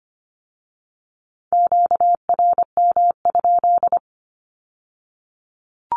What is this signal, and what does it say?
Morse code, heard as a keyed tone, about 25 wpm, 700 Hz. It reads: QRM?
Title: Q-Codes with Spoken Meaning - 25wpm